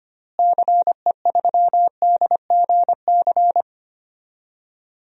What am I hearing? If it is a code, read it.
Morse code, heard as a keyed tone, about 25 wpm, 700 Hz. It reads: CE3DGC
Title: DXCC-100 Call Signs - 6 Characters - Repeated 2x - 25wpm